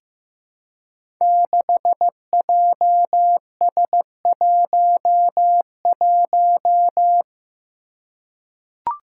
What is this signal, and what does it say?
Morse code, heard as a keyed tone, about 15 wpm, 700 Hz. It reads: 6JS11
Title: Five Challenging Letter-Number Combinations 15wpm